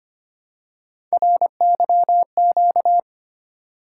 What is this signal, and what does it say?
Morse code, heard as a keyed tone, about 25 wpm, 700 Hz. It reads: RYQ
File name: Three Letter-Number Combinations - 25wpm